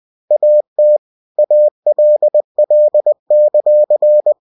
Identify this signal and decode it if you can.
Morse code, heard as a keyed tone, about 20 wpm, 600 Hz. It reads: AT ALL;
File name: cw-books / winnie-the-pooh / winnie-20wpm-page-0005